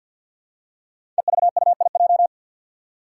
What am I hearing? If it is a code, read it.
Morse code, heard as a keyed tone, about 52 wpm, 700 Hz. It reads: EVUIJ
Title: Five Challenging Letter-Number Combinations 50wpm